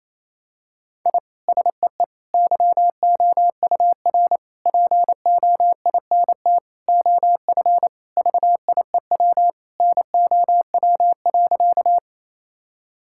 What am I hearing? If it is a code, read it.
Morse code, heard as a keyed tone, about 27 wpm, 700 Hz. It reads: I SEE YOUR POINT OF VIEW NOW.
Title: Sentences from Top 1000 Words - Review Part 6 - 28wpm